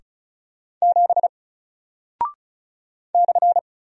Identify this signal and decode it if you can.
Morse code, heard as a keyed tone, about 36 wpm, 700 Hz. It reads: 7 /
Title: Single Character (Letter:Number:Punctuation:Prosign) Alt 35wpm